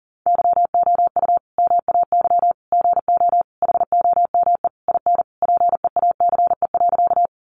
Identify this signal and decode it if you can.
Morse code, heard as a keyed tone, about 40 wpm, 700 Hz. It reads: YOU MAY GO HOME IN PEACE.
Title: Sentences from Top 1500 Words 40wpm